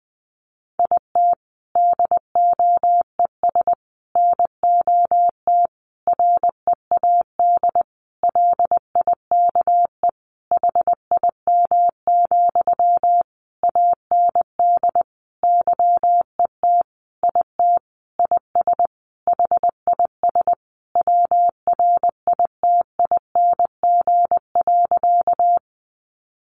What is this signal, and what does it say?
Morse code, heard as a keyed tone, about 20 wpm, 700 Hz. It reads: IT DOES NOT READ LIKE HIM, AND YET IT IS HIS WRITING.
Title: Sentences from Top 800 Words 20wpm